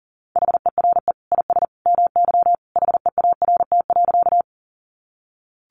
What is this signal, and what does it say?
Morse code, heard as a keyed tone, about 40 wpm, 700 Hz. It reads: HERE IS MY HEART.
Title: Sentences from Top 1200 Words 40wpm